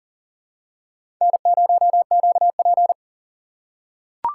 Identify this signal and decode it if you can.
Morse code, heard as a keyed tone, about 39 wpm, 700 Hz. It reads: N0QP
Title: USA Call Signs - 3x Character Spacing - 40wpm